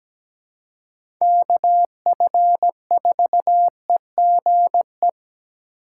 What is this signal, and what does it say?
Morse code, heard as a keyed tone, about 17 wpm, 700 Hz. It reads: KF4EGE
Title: USA Call Signs - 6 Characters - 8x Character Spacing - 17wpm